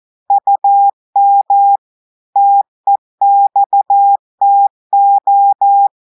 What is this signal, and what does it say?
Morse code, heard as a keyed tone, about 14 wpm, 800 Hz. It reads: UM TEXTO
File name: audios / portugais / portugais texte en clair 1 vitesse 14 mots minutes